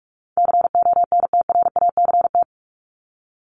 Morse code, heard as a keyed tone, about 44 wpm, 700 Hz. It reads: CONTRACT
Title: Top 1251-1500 Words 45wpm